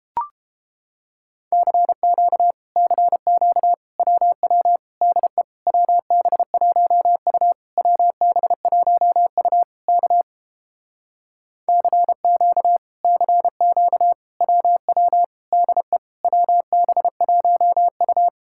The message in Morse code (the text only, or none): CQ CQ WW DE WB1U WB1U K CQ CQ WW DE WB1U